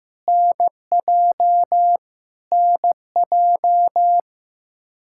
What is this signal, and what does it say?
Morse code, heard as a keyed tone, about 15 wpm, 700 Hz. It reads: NJ NJ
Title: Practice QSO - Exchange 1 - Beginning - 15wpm